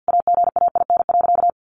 Morse code, heard as a keyed tone, about 48 wpm, 700 Hz. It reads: AGAIN.
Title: Sentences from Top 1000 Words 50wpm